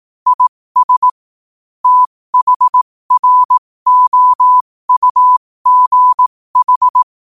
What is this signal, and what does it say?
Morse code, heard as a keyed tone, about 18 wpm, spaced as for 15 wpm, 1000 Hz. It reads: IS THROUGH
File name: qotdfiles / 15wpm